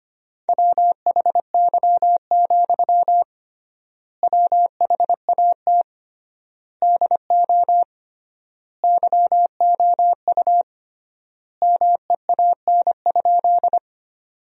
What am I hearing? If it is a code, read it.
Morse code, heard as a keyed tone, about 25 wpm, 700 Hz. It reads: WHY, WHAT DO YOU MEAN?
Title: 3x Word-Spacing - Sentences from Top 200 Words - 25wpm